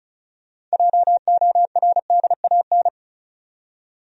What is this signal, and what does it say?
Morse code, heard as a keyed tone, about 36 wpm, 700 Hz. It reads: JORDAN
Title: USA Common Names - 35wpm